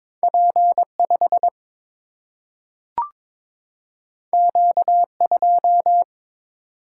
Morse code, heard as a keyed tone, about 22 wpm, 700 Hz. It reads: P5 Q2
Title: Single-Letter Single-Number 22wpm